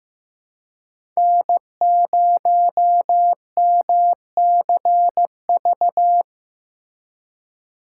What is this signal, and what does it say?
Morse code, heard as a keyed tone, about 15 wpm, 700 Hz. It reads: N0MCV
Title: USA Call Signs - 5 Characters - 3x Character Spacing - 15wpm